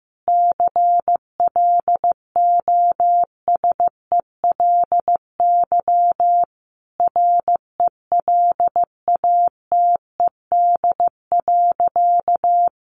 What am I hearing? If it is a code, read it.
Morse code, heard as a keyed tone, about 15 wpm, 700 Hz. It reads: CLOSELY RELATED.
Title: Sentences from Top 1400 Words 15wpm